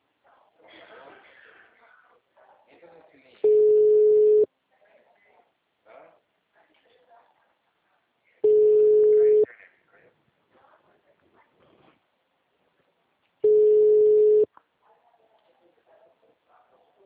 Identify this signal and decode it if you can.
Ringback tone (Europe, 425 Hz), with speech.